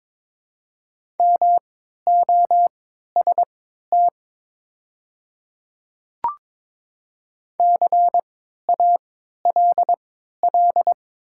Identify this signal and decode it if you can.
Morse code, heard as a keyed tone, about 22 wpm, spaced as for 12 wpm, 700 Hz. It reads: MOST CALL
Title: 3x Character Spacing - Top 100 Words - 22wpm